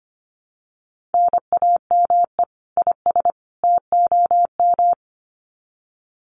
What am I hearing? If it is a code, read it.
Morse code, heard as a keyed tone, about 25 wpm, 700 Hz. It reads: NAME IS TOM
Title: Common QSO Phrases 25wpm